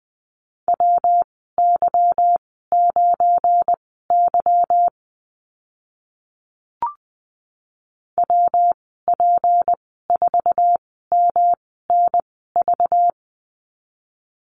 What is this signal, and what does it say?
Morse code, heard as a keyed tone, about 19 wpm, spaced as for 14 wpm, 700 Hz. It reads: WY9Y WP4MNV
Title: 200 Lesson 32g - Letter k - Call Signs 20wpm